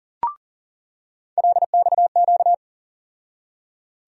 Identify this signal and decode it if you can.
Morse code, heard as a keyed tone, about 40 wpm, 700 Hz. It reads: LXQ
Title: USA Call Sign Postfixes - 3 Letters 40wpm